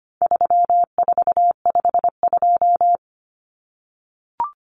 Morse code, heard as a keyed tone, about 25 wpm, 700 Hz. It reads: 3452
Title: Numbers - 4-Digit 25wpm